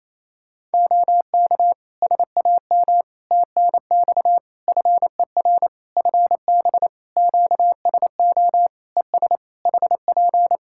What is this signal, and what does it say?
Morse code, heard as a keyed tone, about 28 wpm, 700 Hz. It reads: OK SAM TNX FER FB QSO ES HP
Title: Practice QSO - Exchange 4 - Ending - 28wpm